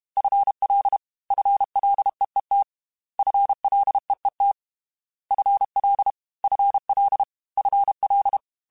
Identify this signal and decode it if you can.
Morse code, heard as a keyed tone, about 32 wpm, 800 Hz. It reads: FL FLEET FLEET FL FL FL
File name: FL / FL mp3s / FL Prefixes 3x2 Twice @30